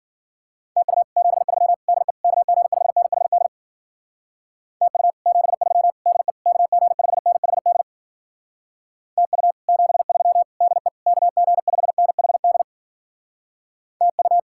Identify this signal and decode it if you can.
Morse code, heard as a keyed tone, about 52 wpm, 700 Hz. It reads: TU 73 DE KG5NHD TU 73 DE KG5NHD TU 73 DE KG5NHD TU